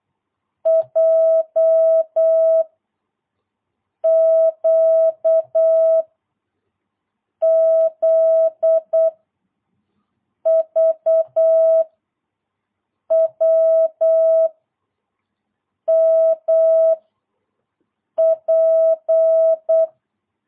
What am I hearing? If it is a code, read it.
Morse code, heard as a keyed tone, about 8 wpm, 650 Hz. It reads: J Q Z V W M P
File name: jh8cbh / 2022cw